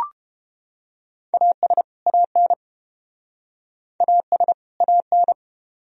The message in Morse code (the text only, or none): AS AN AS AN